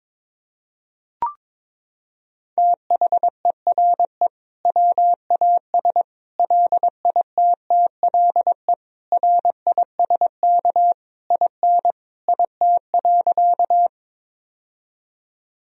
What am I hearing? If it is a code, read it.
Morse code, heard as a keyed tone, about 22 wpm, 700 Hz. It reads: THERE WAS LITTLE RISK IN IT.
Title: Sentences from Top 1000 Words - Review Part 4 - 22wpm